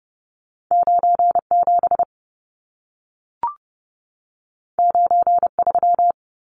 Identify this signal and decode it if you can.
Morse code, heard as a keyed tone, about 30 wpm, 700 Hz. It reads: 97 93
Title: Numbers - 2-Digit 30wpm